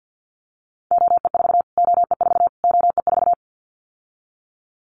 Morse code, heard as a keyed tone, about 50 wpm, 700 Hz. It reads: OE4 OE4 OE4
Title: Callsign Prefixes for Top 25 DXCC - Repeated 3x - 50wpm